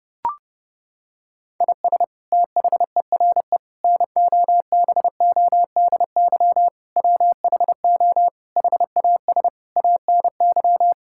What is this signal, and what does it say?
Morse code, heard as a keyed tone, about 30 wpm, 700 Hz. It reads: IS THERE NOBODY WHO HAS ANY
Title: Sentences from Top 1600 Words 30wpm